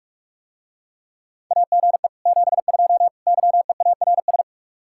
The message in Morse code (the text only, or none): AGE 72 YEARS